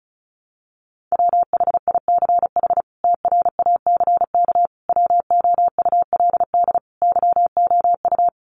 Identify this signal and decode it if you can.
Morse code, heard as a keyed tone, about 34 wpm, 700 Hz. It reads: WHICH TRACK WOULD YOU
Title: Sentences from Top 1100 Words 35wpm